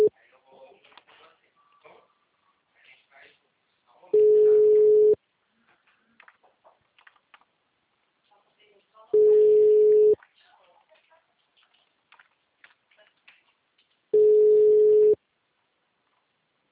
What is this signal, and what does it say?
Ringback tone (Europe, 425 Hz), with speech.
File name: out- -2012- - - .2879